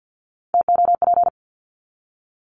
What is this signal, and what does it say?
Morse code, heard as a keyed tone, about 50 wpm, 700 Hz. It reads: TOP
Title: Top 500 Words 50wpm